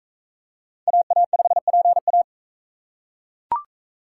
Morse code, heard as a keyed tone, about 41 wpm, 700 Hz. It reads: AA5PA